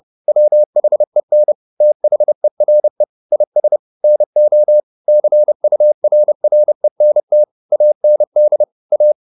Morse code, heard as a keyed tone, about 30 wpm, 600 Hz. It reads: WHEN THERE IS NO CURRENT AND A